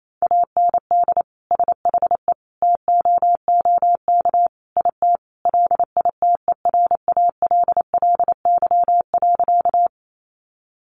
Morse code, heard as a keyed tone, about 28 wpm, 700 Hz. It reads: AND SHE TOOK IT LITERALLY.
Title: Sentences from Top 1400 Words 28wpm